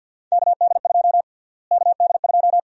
Morse code, heard as a keyed tone, about 51 wpm, 700 Hz. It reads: KD2 KD2
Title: USA Call Sign Prefixes - Repeated 2x - 50wpm